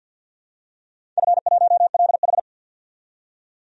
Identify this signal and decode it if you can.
Morse code, heard as a keyed tone, about 47 wpm, 700 Hz. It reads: F1LH
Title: DXCC-100 Call Signs - 4 Characters - 50wpm